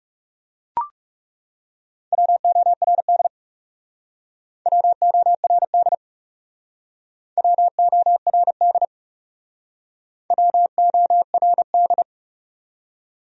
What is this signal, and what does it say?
Morse code, heard as a keyed tone, about 41 wpm, 700 Hz. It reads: WORD WORD WORD WORD